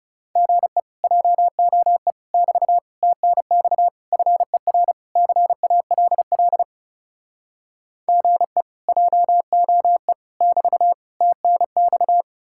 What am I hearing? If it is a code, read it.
Morse code, heard as a keyed tone, about 40 wpm, 700 Hz. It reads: GE JOE <BT> TNX FER CALL GE JOE <BT> TNX